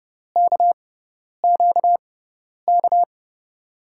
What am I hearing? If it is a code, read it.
Morse code, heard as a keyed tone, about 30 wpm, spaced as for 10 wpm, 700 Hz. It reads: KQK
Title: USA Call Signs - 6x Character Spacing - 30wpm